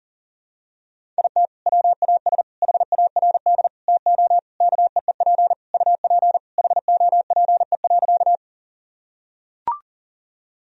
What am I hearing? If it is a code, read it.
Morse code, heard as a keyed tone, about 40 wpm, 700 Hz. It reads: IT WAS HARD TO KEEP UP HOPE.
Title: Sentences from Top 1000 Words - Review Part 1 - 40wpm